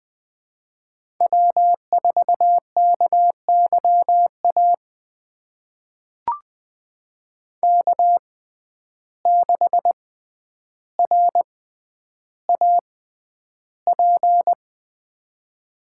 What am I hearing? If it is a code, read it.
Morse code, heard as a keyed tone, about 21 wpm, spaced as for 10 wpm, 700 Hz. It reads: W4KYA K6RAP
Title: USA Call Signs - 5 Characters - 6x Character Spacing - 20wpm